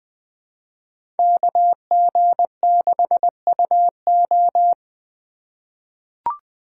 Morse code, heard as a keyed tone, about 20 wpm, 700 Hz. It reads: KG6UO
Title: USA Call Signs - 3x Character Spacing - 20wpm